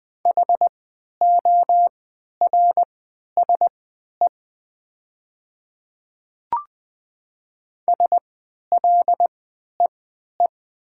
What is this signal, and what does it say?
Morse code, heard as a keyed tone, about 20 wpm, spaced as for 11 wpm, 700 Hz. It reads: HORSE SLEE
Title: 3x Character Spacing - Top 500 Words - 20wpm